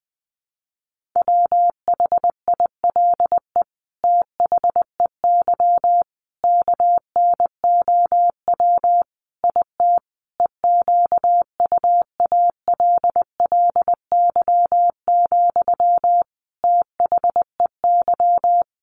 Morse code, heard as a keyed tone, about 20 wpm, 700 Hz. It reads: WHILE THEY KNOW IT EQUALLY, THEY
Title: Sentences from Top 1400 Words 20wpm